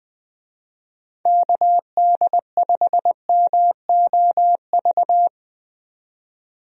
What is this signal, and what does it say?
Morse code, heard as a keyed tone, about 20 wpm, 700 Hz. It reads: KD5MOV
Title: USA Call Signs - 6 Characters - 20wpm